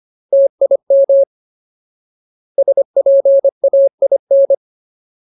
Morse code, heard as a keyed tone, about 25 wpm, 550 Hz. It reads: TIM SPAIN
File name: Ultimate Word List / 25WPM Ultimate